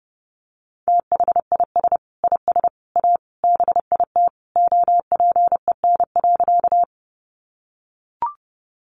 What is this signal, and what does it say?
Morse code, heard as a keyed tone, about 28 wpm, 700 Hz. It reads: THIS IS A BIT OPEN.
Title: Sentences from Top 900 Words 30wpm